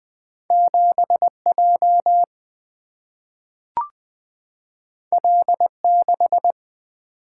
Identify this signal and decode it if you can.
Morse code, heard as a keyed tone, about 20 wpm, 700 Hz. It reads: 7J L6
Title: Single-Letter Single-Number 20wpm